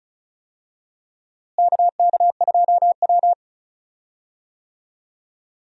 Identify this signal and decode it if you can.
Morse code, heard as a keyed tone, about 34 wpm, 700 Hz. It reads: KK2W
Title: USA Call Signs - 4 Characters - Repeated 2x - 35wpm